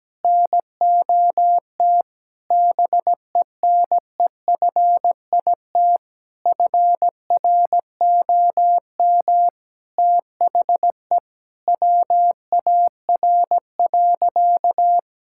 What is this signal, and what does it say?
Morse code, heard as a keyed tone, about 17 wpm, 700 Hz. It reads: NOT BENEFIT FROM THE WAR.
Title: Sentences from Top 1000 Words - Review Part 6 - 17wpm